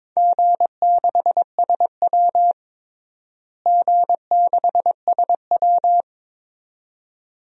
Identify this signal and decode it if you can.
Morse code, heard as a keyed tone, about 22 wpm, 700 Hz. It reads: G6SW G6SW
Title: DXCC-100 Call Signs - 4 Characters - Repeated 2x - 22wpm